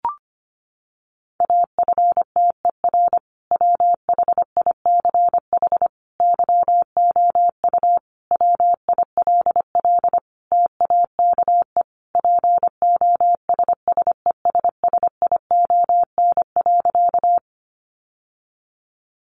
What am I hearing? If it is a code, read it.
Morse code, heard as a keyed tone, about 25 wpm, 700 Hz. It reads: AFTER WHICH YOU WILL TAKE POSSESSION.
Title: Sentences from Top 1400 Words 25wpm